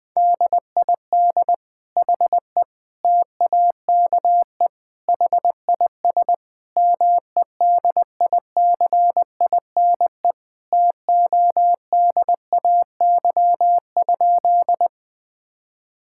Morse code, heard as a keyed tone, about 20 wpm, 700 Hz. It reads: DID HE TAKE HIS MEDICINE TODAY?
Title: Sentences from Top 1000 Words - Review Part 6 - 20wpm